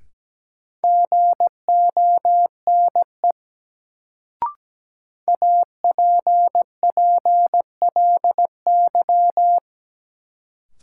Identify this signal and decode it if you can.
Morse code, heard as a keyed tone, about 17 wpm, 700 Hz. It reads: GONE APPLY
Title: Sentences from Top 1200 Words - New Word Drill - 17wpm